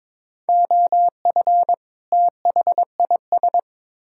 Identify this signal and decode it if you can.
Morse code, heard as a keyed tone, about 21 wpm, 700 Hz. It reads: OF THIS
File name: Build A Phrase - 2-Word Phrases - 22wpm